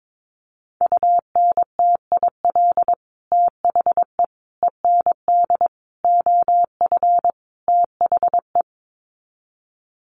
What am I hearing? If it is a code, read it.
Morse code, heard as a keyed tone, about 22 wpm, 700 Hz. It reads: UNTIL THE END OF THE
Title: Sets of 5 Words Encore 22wpm